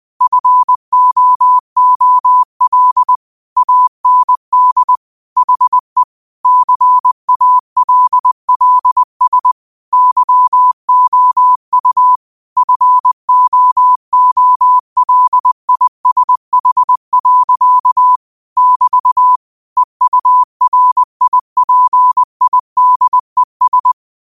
Morse code, heard as a keyed tone, about 20 wpm, 1000 Hz. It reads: FOOL AND HE CALLS YOU FOOLISH. <BT> EURIPIDES